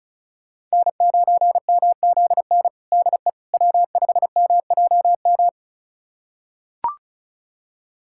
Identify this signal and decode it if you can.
Morse code, heard as a keyed tone, about 34 wpm, 700 Hz. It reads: N9MZN DE W5MJM
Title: USA Callsign DE Callsign - 5 Characters - 35wpm